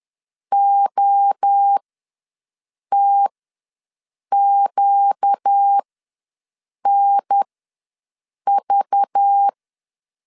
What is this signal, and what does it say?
Morse code, heard as a keyed tone, about 10 wpm, spaced as for 6 wpm, 800 Hz. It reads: OTQNV
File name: LEZIONI / 5 LBVC / 5 LBVC ws 10wpm 3733